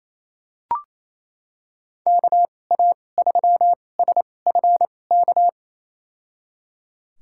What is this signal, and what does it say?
Morse code, heard as a keyed tone, about 28 wpm, spaced as for 20 wpm, 700 Hz. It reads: KA3SFK
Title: USA Call Signs - 6 Characters - 2x Character Spacing - 28wpm